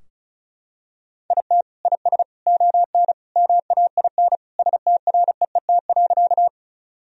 Morse code, heard as a keyed tone, about 36 wpm, 700 Hz. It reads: IT IS ON MAIN STREET.
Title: Sentences from Top 1000 Words - Review Part 2 - 35wpm